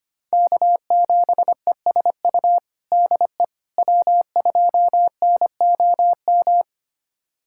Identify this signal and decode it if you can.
Morse code, heard as a keyed tone, about 25 wpm, 700 Hz. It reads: K7ESU DE W2NOM
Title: USA Callsign DE Callsign - 5 Characters - 25wpm